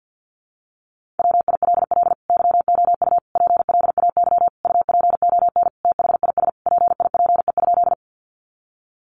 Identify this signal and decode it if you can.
Morse code, heard as a keyed tone, about 50 wpm, 700 Hz. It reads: WILL YOU PLAY UPON THIS PIPE?
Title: Sentences from Top 1500 Words 50wpm